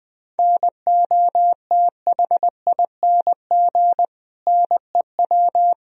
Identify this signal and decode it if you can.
Morse code, heard as a keyed tone, about 20 wpm, 700 Hz. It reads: NOTHING NEW
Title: Sentences from Top 1000 Words - Review Part 1 - 20wpm